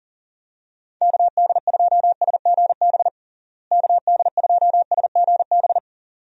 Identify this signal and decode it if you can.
Morse code, heard as a keyed tone, about 40 wpm, 700 Hz. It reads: KD2SGB KD2SGB
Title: USA Call Signs - 6 Characters - Repeated 2x - 40wpm